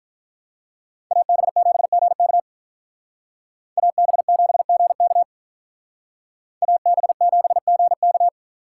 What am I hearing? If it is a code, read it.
Morse code, heard as a keyed tone, about 42 wpm, 700 Hz. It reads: AB7GK AB7GK AB7GK
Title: Speed Racing - USA 2x2 Call Signs 35wpm